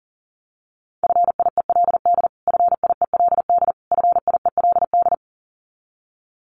Morse code, heard as a keyed tone, about 37 wpm, 700 Hz. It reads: FIELD FIELD FIELD
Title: Top 500 Words Repeated 40wpm